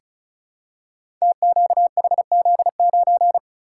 Morse code, heard as a keyed tone, about 36 wpm, 700 Hz. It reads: TQHZ9
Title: Code Groups - 5 Letter:Numbers - 35wpm